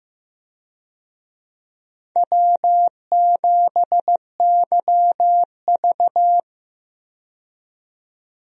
Morse code, heard as a keyed tone, about 15 wpm, 700 Hz. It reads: W7YV
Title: USA Call Signs - 4 Characters - 15wpm